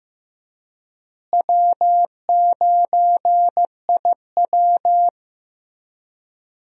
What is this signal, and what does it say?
Morse code, heard as a keyed tone, about 16 wpm, 700 Hz. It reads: W9IW
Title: USA Call Signs - 4 Characters - Repeated 2x - 15wpm